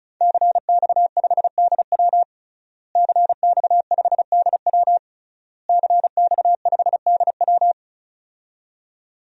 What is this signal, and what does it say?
Morse code, heard as a keyed tone, about 34 wpm, 700 Hz. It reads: CX5DW CX5DW CX5DW
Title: DXCC-100 Call Signs - 5 Characters - Repeated 3x - 35wpm